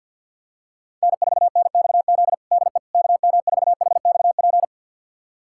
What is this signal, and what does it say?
Morse code, heard as a keyed tone, about 51 wpm, 700 Hz. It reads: N4NXZ DE KM4HXP
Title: USA Callsign DE Callsign 50wpm